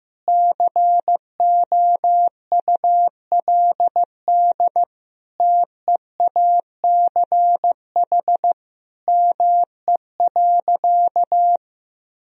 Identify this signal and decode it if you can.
Morse code, heard as a keyed tone, about 15 wpm, 700 Hz. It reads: COULD TEACH ME.
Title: Sentences from Top 1000 Words - Review Part 2 - 15wpm